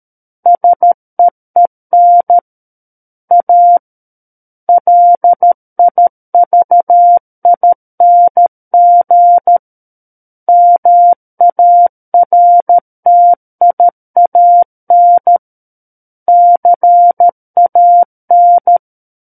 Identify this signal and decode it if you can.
Morse code, heard as a keyed tone, about 13 wpm, 700 Hz. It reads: SEEN A LIVING MARTIAN CAN
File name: Book One / 04-Chapter 4 - 13 WPM